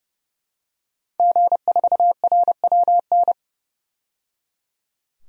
Morse code, heard as a keyed tone, about 30 wpm, 700 Hz. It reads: G4RWN
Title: DXCC-100 Call Signs - 5 Characters - 30wpm